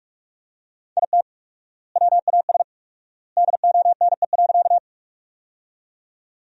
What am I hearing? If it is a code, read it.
Morse code, heard as a keyed tone, about 45 wpm, 700 Hz. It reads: IT WAS DONE.
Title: 4x Word-Spacing - Sentences from Top 500 Words - 45wpm